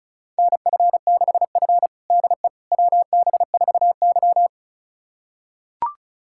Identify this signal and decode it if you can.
Morse code, heard as a keyed tone, about 36 wpm, 700 Hz. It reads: NF6F DE WB4Y
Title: USA Callsign DE Callsign - 4 Characters - 35wpm